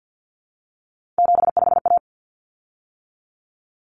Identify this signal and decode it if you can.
Morse code, heard as a keyed tone, about 48 wpm, 700 Hz. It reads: 75A
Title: Common Cut Numbers - 3-Digit 50wpm